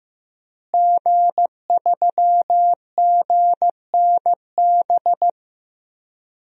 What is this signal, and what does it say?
Morse code, heard as a keyed tone, about 16 wpm, 700 Hz. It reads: G3GNB
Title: DXCC-100 Call Signs - 5 Characters - Repeated 3x - 15wpm